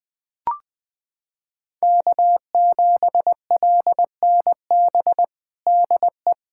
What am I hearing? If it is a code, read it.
Morse code, heard as a keyed tone, about 20 wpm, 700 Hz. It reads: K7LNB DE
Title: USA Callsign DE Callsign 20wpm